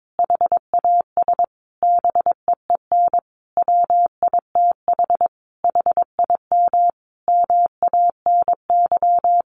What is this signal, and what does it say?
Morse code, heard as a keyed tone, about 22 wpm, 700 Hz. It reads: HAS BEEN WITH HIM MANY
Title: Sentences from Top 200 Words 22wpm